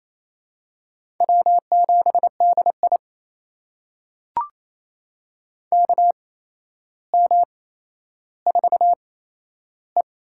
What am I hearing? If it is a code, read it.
Morse code, heard as a keyed tone, about 28 wpm, spaced as for 12 wpm, 700 Hz. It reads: W7DI KM4E
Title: USA Call Signs - 4 Characters - 8x Character Spacing - 28wpm